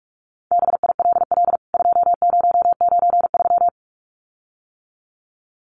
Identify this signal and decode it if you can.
Morse code, heard as a keyed tone, about 44 wpm, 700 Hz. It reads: BILL 2093
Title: CWT Exchange 45wpm